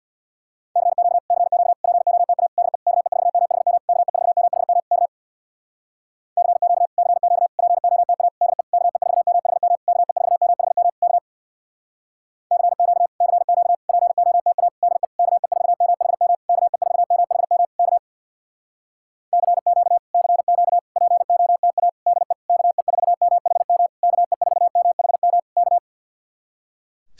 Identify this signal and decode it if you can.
Morse code, heard as a keyed tone, about 64 wpm, 700 Hz. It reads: CQ CQ POTA DE KE4MHM KE4MHM K CQ CQ POTA DE KE4MHM KE4MHM K CQ CQ POTA DE KE4MHM KE4MHM K CQ CQ POTA DE KE4MHM KE4MHM K